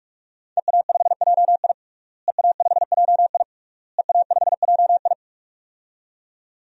Morse code, heard as a keyed tone, about 42 wpm, 700 Hz. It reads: EA5JI EA5JI EA5JI
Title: DXCC-100 Call Signs - 5 Characters - Repeated 3x - 45wpm